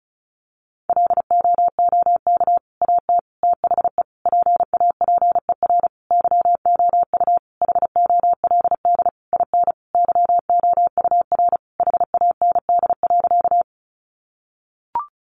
Morse code, heard as a keyed tone, about 34 wpm, 700 Hz. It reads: LOOK AT THE PAPER YOU HOLD IN YOUR HAND.
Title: Sentences from Top 400 Words 35wpm